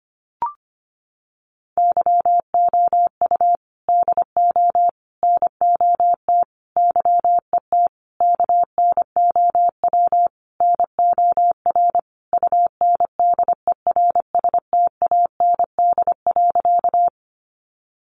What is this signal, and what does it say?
Morse code, heard as a keyed tone, about 25 wpm, 700 Hz. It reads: YOU DO NOT YET KNOW NOR UNDERSTAND.
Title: Sentences from Top 1500 Words 25wpm